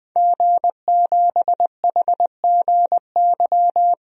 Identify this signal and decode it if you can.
Morse code, heard as a keyed tone, about 21 wpm, 700 Hz. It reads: G7HGY
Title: DXCC-100 Call Signs - 5 Characters - Repeated 2x - 20wpm